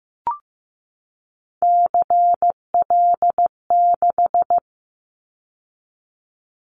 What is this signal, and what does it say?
Morse code, heard as a keyed tone, about 15 wpm, 700 Hz. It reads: CL6
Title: Callsign Prefixes for Top 100 DXCC - 15wpm